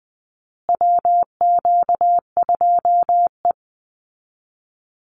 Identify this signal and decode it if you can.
Morse code, heard as a keyed tone, about 20 wpm, 700 Hz. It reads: WQ2E
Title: Call Signs Repeated 2x - 20wpm